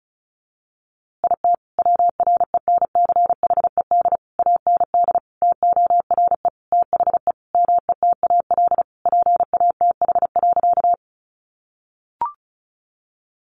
Morse code, heard as a keyed tone, about 34 wpm, 700 Hz. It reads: IT WRENCHED AND TORE THE METAL PATH.
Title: A Sound Of Thunder 35wpm